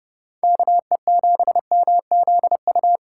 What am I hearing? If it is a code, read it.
Morse code, heard as a keyed tone, about 33 wpm, 700 Hz. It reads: KE7MZU